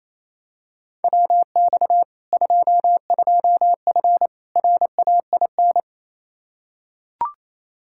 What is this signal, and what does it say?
Morse code, heard as a keyed tone, about 28 wpm, 700 Hz. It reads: WX 22F RAIN
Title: Practice QSO - Exchange 2 - Info - 28wpm